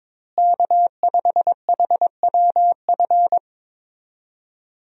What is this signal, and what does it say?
Morse code, heard as a keyed tone, about 22 wpm, 700 Hz. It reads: K5HWF
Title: USA Call Signs - 5 Characters - Repeated 3x - 22wpm